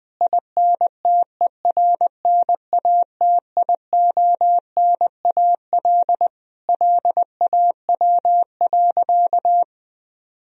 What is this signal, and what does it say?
Morse code, heard as a keyed tone, about 20 wpm, 700 Hz. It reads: INTERNATIONAL LAW.
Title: Sentences from Top 1000 Words - Review Part 4 - 20wpm